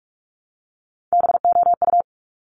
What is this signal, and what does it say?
Morse code, heard as a keyed tone, about 46 wpm, 690 Hz. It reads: BOU
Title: Top Three Letter Combinations in Words 45wpm